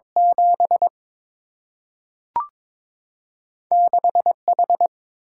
Morse code, heard as a keyed tone, about 23 wpm, 700 Hz. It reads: 7 6 H
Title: Single-Letter Single-Number 22wpm